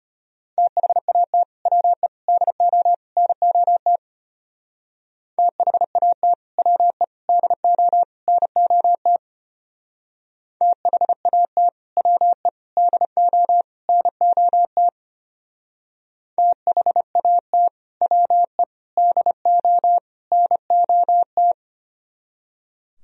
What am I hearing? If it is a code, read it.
Morse code, heard as a keyed tone, about 31 wpm, 700 Hz. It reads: THAT WE DO NOT THAT WE DO NOT THAT WE DO NOT THAT WE DO NOT